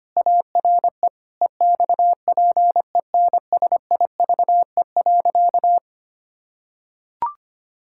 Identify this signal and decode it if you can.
Morse code, heard as a keyed tone, about 25 wpm, 700 Hz. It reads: ARE EXPENSIVE.
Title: Pangram Challenge v2 - 25wpm